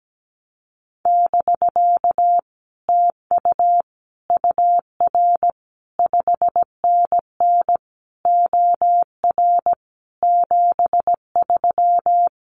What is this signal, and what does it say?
Morse code, heard as a keyed tone, about 17 wpm, 700 Hz. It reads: <BK> TU UR 5NN OR 73
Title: Common QSO Phrases Encore 17wpm